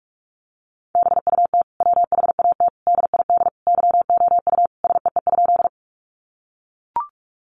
Morse code, heard as a keyed tone, about 44 wpm, 700 Hz. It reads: BUT WHAT DID YOU SEE?
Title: Sentences from Top 100 Words 45wpm